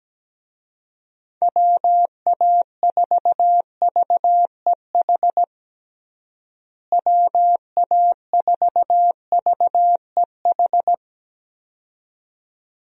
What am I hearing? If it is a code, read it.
Morse code, heard as a keyed tone, about 17 wpm, 700 Hz. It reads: WA4VEH WA4VEH
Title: USA Call Signs - 6 Characters - Repeated 2x - 17wpm